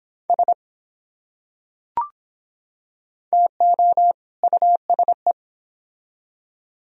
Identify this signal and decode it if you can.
Morse code, heard as a keyed tone, about 24 wpm, 700 Hz. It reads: S TO USE